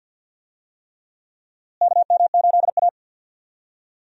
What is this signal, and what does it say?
Morse code, heard as a keyed tone, about 50 wpm, 700 Hz. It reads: KM8A
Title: USA Call Signs - 4 Characters - Repeated 3x - 50wpm